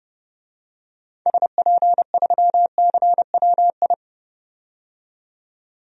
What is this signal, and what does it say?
Morse code, heard as a keyed tone, about 32 wpm, 700 Hz. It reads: SP3CWI
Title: DXCC-100 Call Signs - 6 Characters - 30wpm